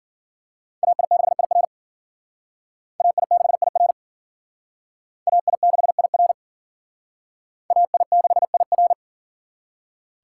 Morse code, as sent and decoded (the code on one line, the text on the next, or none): .- .. -.... .. .-. / .- .. -.... .. .-. / .- .. -.... .. .-. / .- .. -.... .. .-.
AI6IR AI6IR AI6IR AI6IR